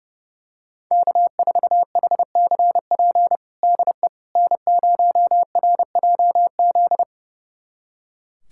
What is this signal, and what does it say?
Morse code, heard as a keyed tone, about 32 wpm, 700 Hz. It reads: K4HCP DE N0RJZ
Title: USA Callsign DE Callsign - 5 Characters - 30wpm